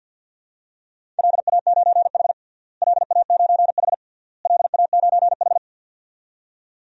Morse code, heard as a keyed tone, about 51 wpm, 700 Hz. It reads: LA9H LA9H LA9H
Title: DXCC-100 Call Signs - 4 Characters - Repeated 3x - 50wpm